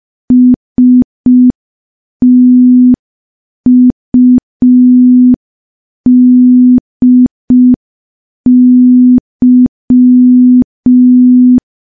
Morse code, heard as a keyed tone, about 5 wpm, 250 Hz. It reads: STUDY
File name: morse / morse-training-05WPM